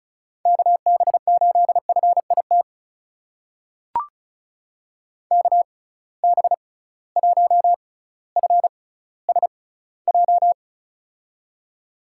Morse code, heard as a keyed tone, about 36 wpm, spaced as for 18 wpm, 700 Hz. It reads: KB8FIT KB1FSJ
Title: USA Call Signs - 6 Characters - 6x Character Spacing - 35wpm